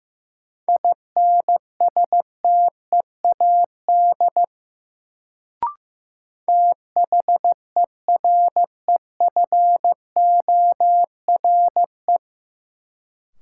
Morse code, heard as a keyed tone, about 15 wpm, 700 Hz. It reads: INSTEAD THEREFORE
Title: Sentences from Top 800 Words - New Word Drill - 15wpm